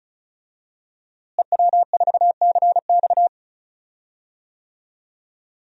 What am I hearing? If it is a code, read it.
Morse code, heard as a keyed tone, about 36 wpm, 700 Hz. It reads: EW4CX
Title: DXCC-100 Call Signs - 5 Characters - Repeated 2x - 35wpm